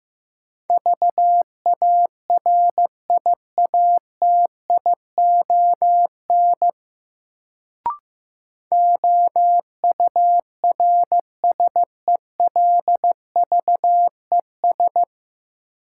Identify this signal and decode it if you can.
Morse code, heard as a keyed tone, about 16 wpm, 700 Hz. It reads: VARIATION OURSELVES
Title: Sentences from Top 1600 Words - New Word Drill - 15wpm